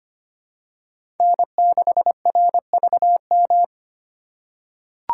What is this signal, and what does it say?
Morse code, heard as a keyed tone, about 25 wpm, 700 Hz. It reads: N6RVM
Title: USA Call Signs - 5 Characters - 4x Character Spacing - 25wpm